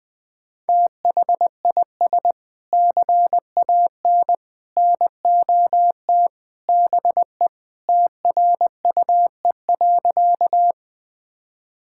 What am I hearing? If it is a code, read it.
Morse code, heard as a keyed tone, about 20 wpm, 700 Hz. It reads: THIS CAN NOT BE TRUE.
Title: Sentences with 3-Word Phrases - 20wpm